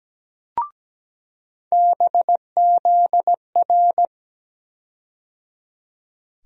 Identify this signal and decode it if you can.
Morse code, heard as a keyed tone, about 17 wpm, 700 Hz. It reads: BZR
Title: USA Call Sign Postfixes - 3 Letters 17wpm